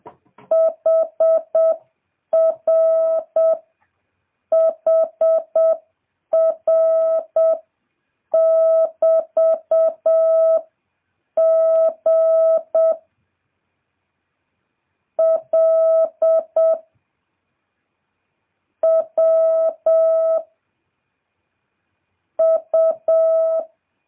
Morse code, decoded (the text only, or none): HR HR <BT> G L W U